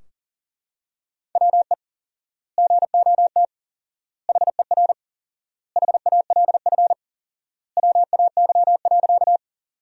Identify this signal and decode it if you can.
Morse code, heard as a keyed tone, about 40 wpm, 700 Hz. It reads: WE GOT HER HALF WAY.
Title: 4x Word-Spacing - Sentences from Top 400 Words - 40wpm